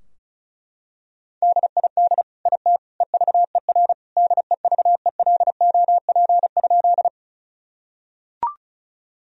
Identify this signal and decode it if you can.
Morse code, heard as a keyed tone, about 36 wpm, 700 Hz. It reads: DID IT EVER DEVELOP?
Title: Sentences from Top 1000 Words - Review Part 2 - 35wpm